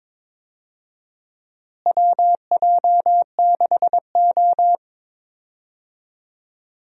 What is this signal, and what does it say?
Morse code, heard as a keyed tone, about 22 wpm, 700 Hz. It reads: WJ6O
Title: USA Call Signs - 4 Characters - Repeated 2x - 22wpm